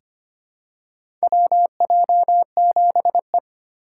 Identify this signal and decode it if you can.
Morse code, heard as a keyed tone, about 25 wpm, 690 Hz. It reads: WJ7E
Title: USA Call Signs - 4 Characters - 4x Character Spacing - 25wpm